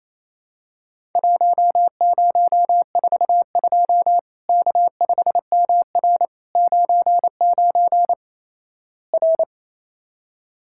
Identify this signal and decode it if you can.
Morse code, heard as a keyed tone, about 27 wpm, 700 Hz. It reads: 1042 K5MR 99 R
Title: Worked All Europe DX Contest - QTC Exchange - 28wpm